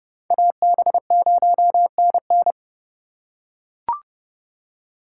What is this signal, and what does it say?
Morse code, heard as a keyed tone, about 30 wpm, 700 Hz. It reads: AB0NN